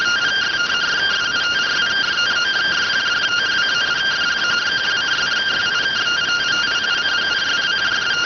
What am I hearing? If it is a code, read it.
Two-tone teleprinter signal, about 170 Hz shift.